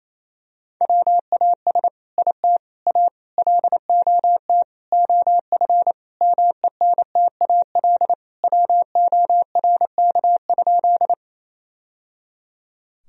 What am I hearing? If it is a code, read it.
Morse code, heard as a keyed tone, about 28 wpm, 700 Hz. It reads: WAS IT A LOT OF MENTAL WORK?
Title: Sentences from Top 1000 Words - Review Part 4 - 28wpm